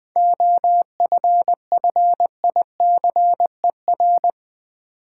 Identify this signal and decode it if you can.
Morse code, heard as a keyed tone, about 20 wpm, 700 Hz. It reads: OFFICER
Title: Seven Letter Words - 20wpm